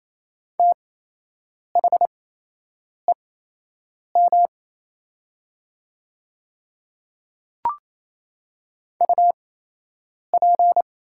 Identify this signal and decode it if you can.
Morse code, heard as a keyed tone, about 28 wpm, spaced as for 8 wpm, 700 Hz. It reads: THEM UP
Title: 8x Character Spacing - Top 100 Words - 28wpm